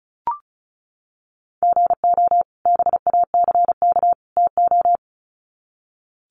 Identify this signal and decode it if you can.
Morse code, heard as a keyed tone, about 34 wpm, 700 Hz. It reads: GO BACK TO
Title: Sets of 3 Words 35wpm